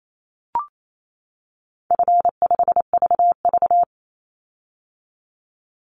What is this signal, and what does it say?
Morse code, heard as a keyed tone, about 28 wpm, 700 Hz. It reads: F5VV
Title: Easy Call Signs - 28wpm